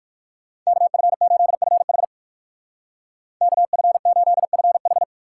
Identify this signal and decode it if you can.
Morse code, heard as a keyed tone, about 57 wpm, 700 Hz. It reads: KF8FH KF8FH